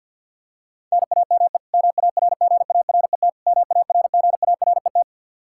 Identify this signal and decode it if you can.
Morse code, heard as a keyed tone, about 52 wpm, 700 Hz. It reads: NAME MARGARET MARGARET
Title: Practice QSO - Exchange 1 - Beginning - 50wpm